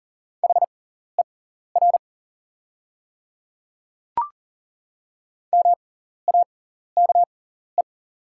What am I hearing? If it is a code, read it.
Morse code, heard as a keyed tone, about 38 wpm, spaced as for 14 wpm, 700 Hz. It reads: HER MAKE
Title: 6x Character Spacing - Top 100 Words - 40wpm